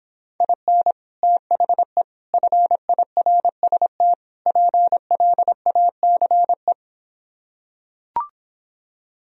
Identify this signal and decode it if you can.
Morse code, heard as a keyed tone, about 26 wpm, 700 Hz. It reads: IN THE FIRST PLACE